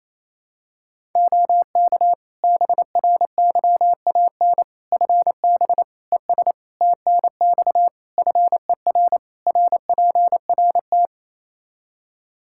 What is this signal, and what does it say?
Morse code, heard as a keyed tone, about 27 wpm, 700 Hz. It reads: OK BRYAN FB ES TNX FER RPRT
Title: Practice QSO - Exchange 2 - Info - 28wpm